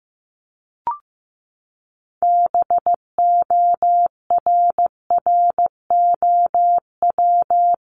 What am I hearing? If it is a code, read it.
Morse code, heard as a keyed tone, about 15 wpm, 700 Hz. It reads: BORROW
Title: Top 1251-1500 Words 15wpm